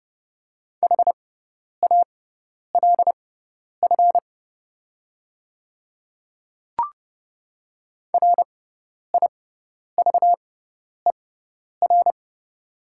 Morse code, heard as a keyed tone, about 30 wpm, spaced as for 10 wpm, 700 Hz. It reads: HALF RIVER
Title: 6x Character Spacing - Top 500 Words - 30wpm